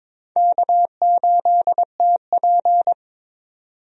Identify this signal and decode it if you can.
Morse code, heard as a keyed tone, about 22 wpm, 700 Hz. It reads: K8TP
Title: USA Call Signs - 4 Characters - 22wpm